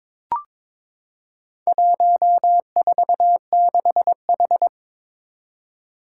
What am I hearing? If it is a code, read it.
Morse code, heard as a keyed tone, about 22 wpm, 700 Hz. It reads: 146H
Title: Four Challenging Letter-Number Combinations 22wpm